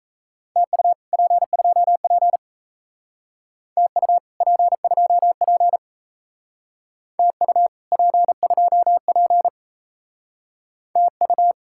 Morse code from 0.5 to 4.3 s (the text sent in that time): TU P2P TU